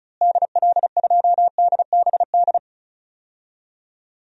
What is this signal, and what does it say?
Morse code, heard as a keyed tone, about 34 wpm, 700 Hz. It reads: DL2DBD
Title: DXCC-100 Call Signs - 6 Characters - Repeated 2x - 35wpm